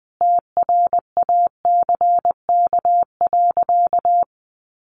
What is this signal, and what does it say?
Morse code, heard as a keyed tone, about 20 wpm, 700 Hz. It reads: TRACK.
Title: Sentences from Top 1100 Words 20wpm